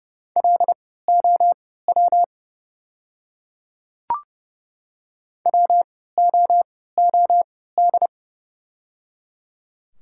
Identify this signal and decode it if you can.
Morse code, heard as a keyed tone, about 30 wpm, spaced as for 17 wpm, 700 Hz. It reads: LOW WOOD
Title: 3x Character Spacing - Top 300 Words - 30wpm